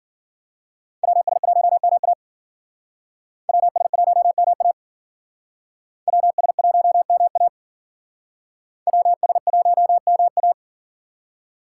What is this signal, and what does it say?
Morse code, heard as a keyed tone, about 53 wpm, 700 Hz. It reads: WS1MA WS1MA WS1MA WS1MA